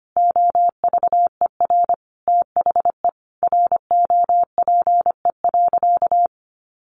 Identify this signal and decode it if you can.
Morse code, heard as a keyed tone, about 25 wpm, 700 Hz. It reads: OVER THE ROPE.
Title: Easy Phrases 25wpm